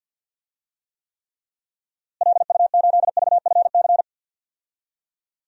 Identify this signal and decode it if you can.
Morse code, heard as a keyed tone, about 47 wpm, 700 Hz. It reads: LU8VFC
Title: DXCC-100 Call Signs - 6 Characters - Repeated 3x - 50wpm